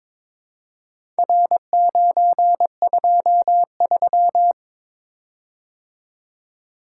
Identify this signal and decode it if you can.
Morse code, heard as a keyed tone, about 22 wpm, 700 Hz. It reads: R923
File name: practice / 22wpm / Four Letter-Number Combinations - 22wpm